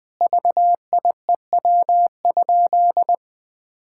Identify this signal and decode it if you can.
Morse code, heard as a keyed tone, about 20 wpm, 700 Hz. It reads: VIEW?
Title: Sentences from Top 1000 Words - Review Part 6 - 20wpm